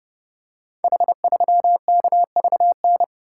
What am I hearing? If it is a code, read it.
Morse code, heard as a keyed tone, about 31 wpm, 700 Hz. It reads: H3KVN
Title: Code Groups - 5 Letter:Numbers - 30wpm